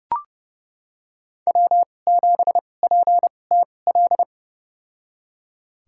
Morse code, heard as a keyed tone, about 30 wpm, spaced as for 22 wpm, 700 Hz. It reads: W7PTL
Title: USA Call Signs - 5 Characters - 2x Character Spacing - 30wpm